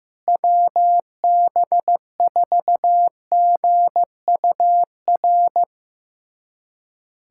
Morse code, heard as a keyed tone, about 15 wpm, 700 Hz. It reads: WB4GUR
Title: USA Callsign DE Callsign 15wpm